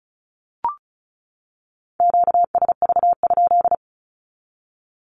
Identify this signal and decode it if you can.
Morse code, heard as a keyed tone, about 34 wpm, 700 Hz. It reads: QSV?
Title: Q-Codes 35wpm